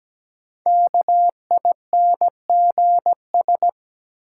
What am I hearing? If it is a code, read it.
Morse code, heard as a keyed tone, about 17 wpm, 700 Hz. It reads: KINGS
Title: Sentences from Top 1000 Words - Review Part 6 - New Word Drill - 17wpm